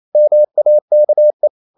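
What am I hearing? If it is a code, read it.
Morse code, heard as a keyed tone, about 28 wpm, 600 Hz. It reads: MAKE
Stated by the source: Morse code practice at 28 wpm, text of Ken Follett's novel Never in English, from CW med Gnister.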